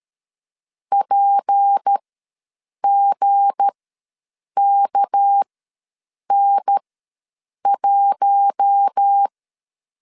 Morse code, heard as a keyed tone, about 13 wpm, spaced as for 7 wpm, 800 Hz. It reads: PGKN1
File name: LEZIONI / 7 GF9Z / 7 G ws 12wpm 3140